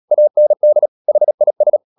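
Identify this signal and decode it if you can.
Morse code, heard as a keyed tone, about 37 wpm, 600 Hz. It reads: AND HIS
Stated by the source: Morse code practice at 35–39 wpm sending English text of Ken Follett's novel Never, from CW med Gnister.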